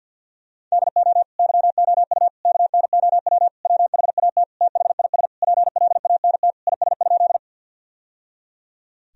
Morse code, heard as a keyed tone, about 51 wpm, 700 Hz. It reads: DO YOU KNOW WHAT THIS PLANT IS?
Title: Sentences from Top 1000 Words - Review Part 1 - 50wpm